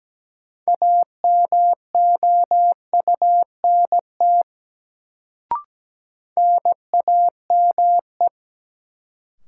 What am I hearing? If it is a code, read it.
Morse code, heard as a keyed tone, about 17 wpm, 700 Hz. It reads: AMOUNT NAME
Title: Sets of 3 Words - New Word Drill - 17wpm